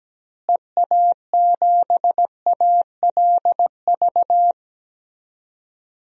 Morse code, heard as a keyed tone, about 17 wpm, 700 Hz. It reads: EA7ALV
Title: DXCC-100 Call Signs - 6 Characters - Repeated 3x - 17wpm